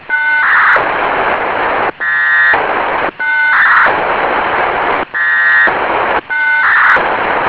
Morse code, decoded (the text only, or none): none